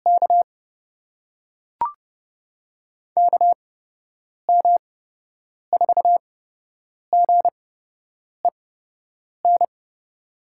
Morse code, heard as a keyed tone, about 30 wpm, spaced as for 8 wpm, 700 Hz. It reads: K KM4GEN
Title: USA Call Signs - 6 Characters - 8x Character Spacing - 30wpm